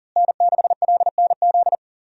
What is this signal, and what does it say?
Morse code, heard as a keyed tone, about 39 wpm, 700 Hz. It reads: N6LNZ
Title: USA Call Signs - 5 Characters - Repeated 2x - 40wpm